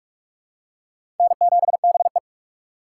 Morse code, heard as a keyed tone, about 45 wpm, 700 Hz. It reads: N7BE
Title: USA Call Signs - 4 Characters - 3x Character Spacing - 45wpm